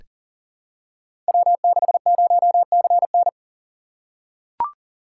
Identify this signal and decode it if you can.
Morse code, heard as a keyed tone, about 42 wpm, 700 Hz. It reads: W60CN
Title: Code Groups - 5 Letter:Numbers - 40wpm